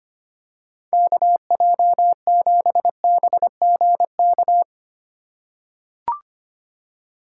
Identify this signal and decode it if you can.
Morse code, heard as a keyed tone, about 25 wpm, 700 Hz. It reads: KJ7BGK
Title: USA Call Signs - 6 Characters - 4x Character Spacing - 25wpm